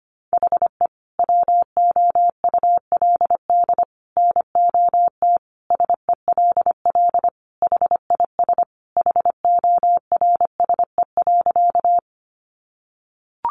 Morse code, heard as a keyed tone, about 25 wpm, 700 Hz. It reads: HE WOULD NOT SELL HIS HORSE.
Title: Sentences from Top 1100 Words 25wpm